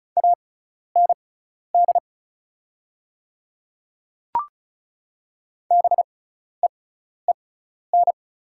Morse code, heard as a keyed tone, about 36 wpm, spaced as for 12 wpm, 700 Hz. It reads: AND BEEN
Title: 6x Character Spacing - Top 100 Words - 35wpm